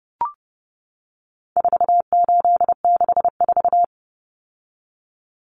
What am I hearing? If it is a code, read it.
Morse code, heard as a keyed tone, about 30 wpm, 700 Hz. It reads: 4864
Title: Numbers - 4-Digit 30wpm